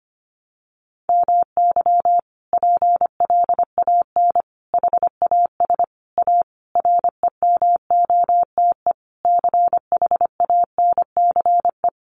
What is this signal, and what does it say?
Morse code, heard as a keyed tone, about 25 wpm, 700 Hz. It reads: MY PLAN HAS A REMOTE CHANCE
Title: Sentences from Top 1300 Words 25wpm